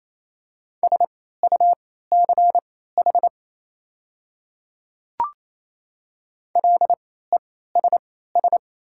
Morse code, heard as a keyed tone, about 28 wpm, spaced as for 16 wpm, 700 Hz. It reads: SUCH LESS
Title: 3x Character Spacing - Top 500 Words - 28wpm